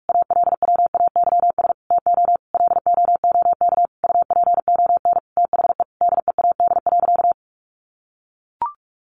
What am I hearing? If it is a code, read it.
Morse code, heard as a keyed tone, about 44 wpm, 700 Hz. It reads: ALWAYS TO LOOK UPON THE DEAD.
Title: Sentences from Top 1500 Words 45wpm